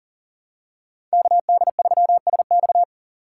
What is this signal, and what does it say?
Morse code, heard as a keyed tone, about 40 wpm, 700 Hz. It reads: KD3SX
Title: USA Call Signs - 5 Characters - 3x Character Spacing - 40wpm